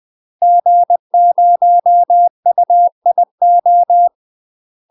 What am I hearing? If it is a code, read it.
Morse code, heard as a keyed tone, about 20 wpm, 700 Hz. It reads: G0UIO
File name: UK MASTER QSOs TRAINING / 20WPM / UK MASTER QSOs 0044